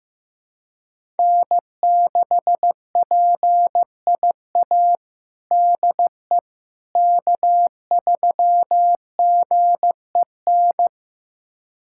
Morse code, heard as a keyed tone, about 15 wpm, 700 Hz. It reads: N6PIA DE K3GEN
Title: USA Callsign DE Callsign - 5 Characters - 15wpm